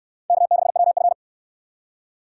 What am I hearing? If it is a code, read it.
Morse code, heard as a keyed tone, about 62 wpm, 700 Hz. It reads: K6WF